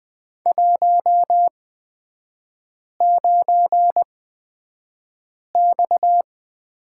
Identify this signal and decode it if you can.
Morse code, heard as a keyed tone, about 20 wpm, spaced as for 5 wpm, 700 Hz. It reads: 19X
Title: 236 Lesson 38e - Letter q - Three Characters 20wpm